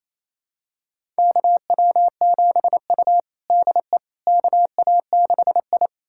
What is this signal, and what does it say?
Morse code, heard as a keyed tone, about 28 wpm, 700 Hz. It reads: KW7U DE KA6I
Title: USA Callsign DE Callsign - 4 Characters - 28wpm